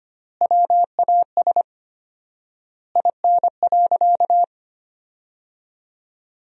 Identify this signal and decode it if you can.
Morse code, heard as a keyed tone, about 25 wpm, 700 Hz. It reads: WAS IN.
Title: 4x Word-Spacing - Sentences from Top 400 Words - 25wpm